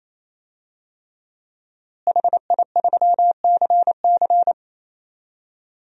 Morse code, heard as a keyed tone, about 28 wpm, 700 Hz. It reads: HI3CC
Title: DXCC-100 Call Signs - 5 Characters - Repeated 3x - 28wpm